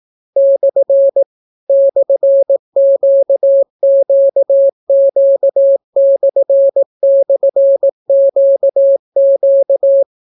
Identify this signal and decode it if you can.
Morse code, heard as a keyed tone, about 18 wpm, 550 Hz. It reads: / /QQQ//QQ